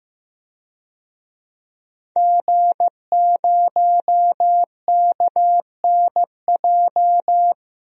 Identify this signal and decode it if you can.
Morse code, heard as a keyed tone, about 15 wpm, 700 Hz. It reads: G0KNJ
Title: DXCC-100 Call Signs - 5 Characters - 15wpm